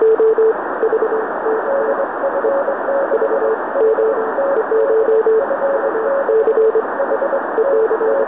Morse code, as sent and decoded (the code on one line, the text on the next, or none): none